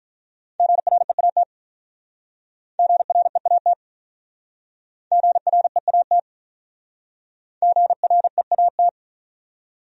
Morse code, heard as a keyed tone, about 44 wpm, 700 Hz. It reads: GREAT GREAT GREAT GREAT